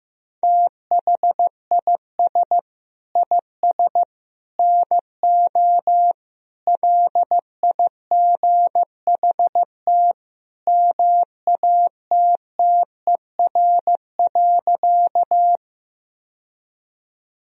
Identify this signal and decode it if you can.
Morse code, heard as a keyed tone, about 15 wpm, 700 Hz. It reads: THIS IS NO LIGHT MATTER.
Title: Sentences from Top 1000 Words - Review Part 1 - 15wpm